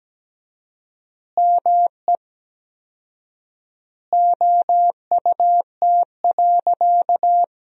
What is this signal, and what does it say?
Morse code, heard as a keyed tone, about 17 wpm, 700 Hz. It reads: ME OUT.
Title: 4x Word-Spacing - Sentences from Top 500 Words - 17wpm